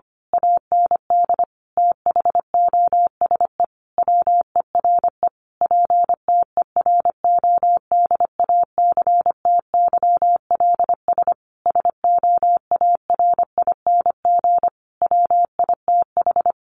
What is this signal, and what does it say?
Morse code, heard as a keyed tone, about 25 wpm, 700 Hz. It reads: AND THOSE WERE PTERODACTYLS SOARING WITH
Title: A Sound Of Thunder 25wpm